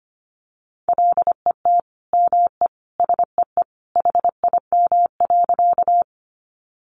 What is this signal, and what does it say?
Morse code, heard as a keyed tone, about 25 wpm, 700 Hz. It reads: LET ME SEE HIM.
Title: Sentences from Top 300 Words 25wpm